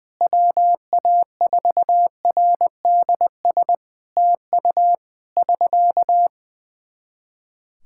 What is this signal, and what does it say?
Morse code, heard as a keyed tone, about 20 wpm, 700 Hz. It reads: WA4RDS TU <SK>
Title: Practice QSO - Exchange 4 - Ending - 20wpm